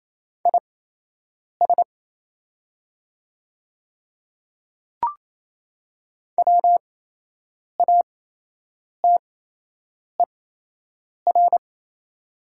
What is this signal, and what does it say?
Morse code, heard as a keyed tone, about 28 wpm, spaced as for 8 wpm, 700 Hz. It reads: IS WATER
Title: 8x Character Spacing - Top 300 Words - 28wpm